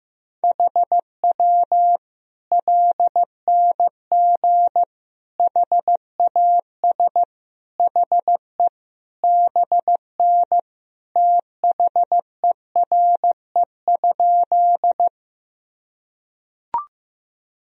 Morse code, heard as a keyed tone, about 15 wpm, 700 Hz. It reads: HW LNG HAS HE BN THERE?
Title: Sentences from Top 100 Words and Abbreviations 15wpm